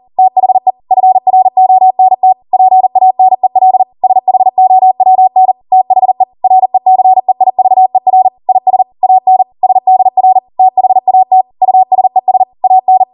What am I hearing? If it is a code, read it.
Morse code, heard as a keyed tone, about 40 wpm, 750 Hz. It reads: THE FRONT PANEL SHOWN THE RECEIVER IS AN SDR THAT USES AN